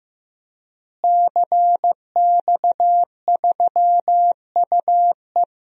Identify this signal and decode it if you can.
Morse code, heard as a keyed tone, about 15 wpm, 700 Hz. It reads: CX3UE
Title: DXCC-100 Call Signs - 5 Characters - Repeated 3x - 15wpm